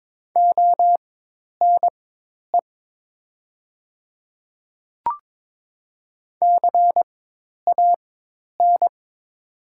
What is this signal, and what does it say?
Morse code, heard as a keyed tone, about 22 wpm, spaced as for 10 wpm, 700 Hz. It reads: ONE CAN
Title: 4x Character Spacing - Top 100 Words - 22wpm